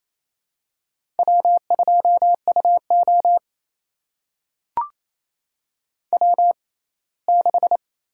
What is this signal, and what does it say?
Morse code, heard as a keyed tone, about 28 wpm, spaced as for 19 wpm, 700 Hz. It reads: W2UO W6
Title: USA Call Signs - 4 Characters - 6x Character Spacing - 28wpm